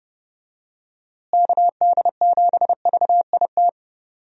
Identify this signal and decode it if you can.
Morse code, heard as a keyed tone, about 30 wpm, 700 Hz. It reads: KD7VIT
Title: USA Call Signs - 6 Characters - Repeated 2x - 30wpm